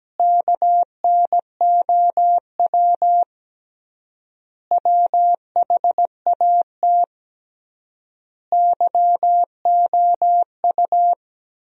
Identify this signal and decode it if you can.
Morse code, heard as a keyed tone, about 17 wpm, 700 Hz. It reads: KNOW WHAT YOU
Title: 3x Word-Spacing - Sentences from Top 100 Words - 17wpm